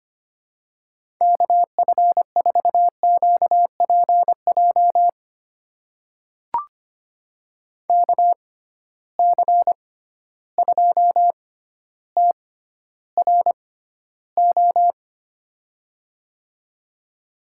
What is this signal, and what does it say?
Morse code, heard as a keyed tone, about 25 wpm, spaced as for 13 wpm, 700 Hz. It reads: KF4QPJ KC2TRO
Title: USA Call Signs - 6 Characters - 6x Character Spacing - 25wpm